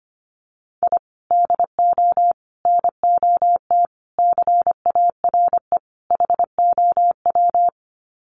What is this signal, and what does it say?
Morse code, heard as a keyed tone, about 25 wpm, 700 Hz. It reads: I DO NOT CARE HOW
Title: Sets of 5 Words Encore2 25wpm